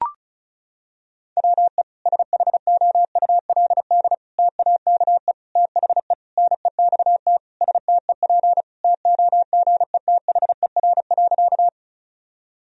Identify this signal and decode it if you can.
Morse code, heard as a keyed tone, about 36 wpm, 700 Hz. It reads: WE SHOULD TAKE THE NEXT STEP TOGETHER.
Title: Sentences from Top 1000 Words - Review Part 2 - 35wpm